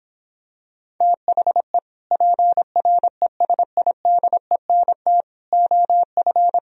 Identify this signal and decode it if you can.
Morse code, heard as a keyed tone, about 26 wpm, 700 Hz. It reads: THE PRESIDENT OF